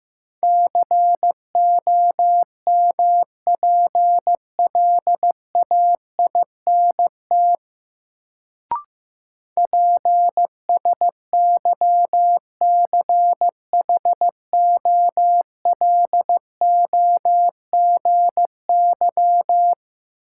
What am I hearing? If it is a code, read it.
Morse code, heard as a keyed tone, about 15 wpm, 700 Hz. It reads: COMPLAINT PSYCHOLOGY
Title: Sentences from Top 1400 Words - New Word Drill - 15wpm